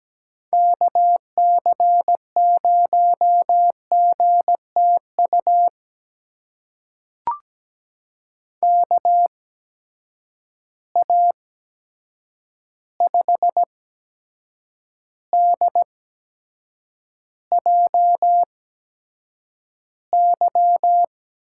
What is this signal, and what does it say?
Morse code, heard as a keyed tone, about 17 wpm, spaced as for 7 wpm, 700 Hz. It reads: KC0GTU KA5DJY
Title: USA Call Signs - 6 Characters - 8x Character Spacing - 17wpm